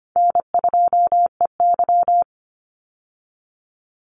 Morse code, heard as a keyed tone, about 24 wpm, 700 Hz. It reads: N2EY
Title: Call Signs Repeated 3x - 25wpm